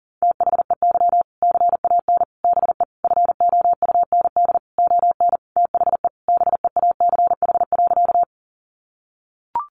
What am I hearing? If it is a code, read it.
Morse code, heard as a keyed tone, about 40 wpm, 700 Hz. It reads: THEY CAN BE FOUND ON THE BEACH.
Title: Sentences from Top 1500 Words 40wpm